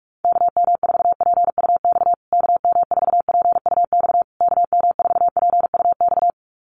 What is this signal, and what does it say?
Morse code, heard as a keyed tone, about 44 wpm, 700 Hz. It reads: KM4PUX KM4PUX KM4PUX
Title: USA Call Signs Repeated 3x 45wpm